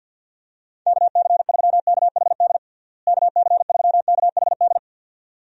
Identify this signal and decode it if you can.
Morse code, heard as a keyed tone, about 46 wpm, 700 Hz. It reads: KC3KHD KC3KHD
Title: USA Call Signs - 6 Characters - Repeated 2x - 50wpm